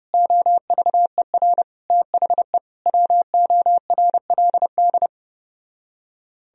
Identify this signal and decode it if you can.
Morse code, heard as a keyed tone, about 33 wpm, 700 Hz. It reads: OVER THE WORLD